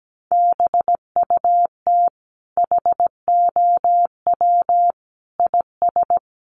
Morse code, heard as a keyed tone, about 17 wpm, 700 Hz. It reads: BUT HOW IS
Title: Sentences from Top 100 Words 17wpm